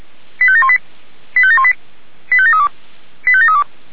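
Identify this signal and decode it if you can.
Five-tone selective call (ZVEI), heard as quick runs of five tones, tones of about 70 ms. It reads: 85718 85718 85732 85732
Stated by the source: ZVEI five-tone call received on SDR.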